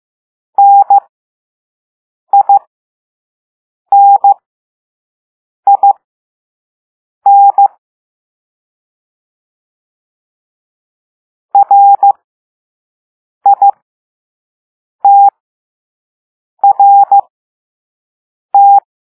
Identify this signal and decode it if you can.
Morse code, heard as a keyed tone, about 15 wpm, spaced as for 5 wpm, 800 Hz. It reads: NININ RITRT
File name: Morse / Level2-All-A